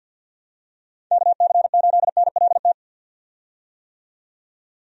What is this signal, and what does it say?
Morse code, heard as a keyed tone, about 50 wpm, 700 Hz. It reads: KC8NLT
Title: USA Call Signs - 6 Characters - 50wpm